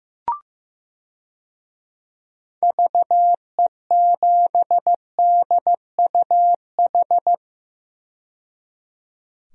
Morse code, heard as a keyed tone, about 15 wpm, 700 Hz. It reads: VE7DUH
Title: DXCC-100 Call Signs - 6 Characters - 15wpm